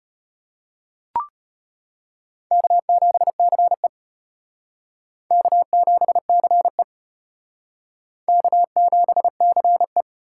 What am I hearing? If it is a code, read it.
Morse code, heard as a keyed tone, about 32 wpm, 700 Hz. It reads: K7CE K7CE K7CE